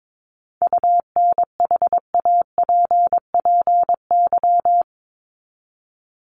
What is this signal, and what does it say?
Morse code, heard as a keyed tone, about 22 wpm, 700 Hz. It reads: UNHAPPY
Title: Top 1501-1750 Words 22wpm